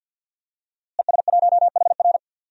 Morse code, heard as a keyed tone, about 51 wpm, 700 Hz. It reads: ES1HR
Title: DXCC-100 Call Signs - 5 Characters - Repeated 3x - 50wpm